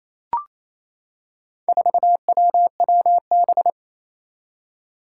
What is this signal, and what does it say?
Morse code, heard as a keyed tone, about 28 wpm, 700 Hz. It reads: 4WWB
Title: Four Challenging Letter-Number Combinations 28wpm